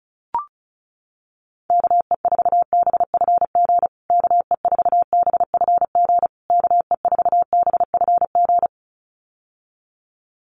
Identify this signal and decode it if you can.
Morse code, heard as a keyed tone, about 34 wpm, 700 Hz. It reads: KE4BFG KE4BFG KE4BFG
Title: USA Call Signs Repeated 3x 35wpm